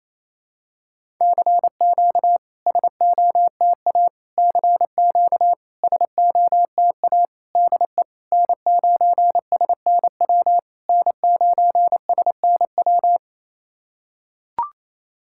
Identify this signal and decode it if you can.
Morse code, heard as a keyed tone, about 28 wpm, 700 Hz. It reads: CQ SOTA CQ SOTA DE N9SNW N9SNW
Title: SOTA QSO Practice 28wpm